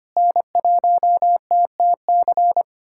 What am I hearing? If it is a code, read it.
Morse code, heard as a keyed tone, about 25 wpm, 700 Hz. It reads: N1TTC
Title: USA Callsign DE Callsign - 5 Characters - 25wpm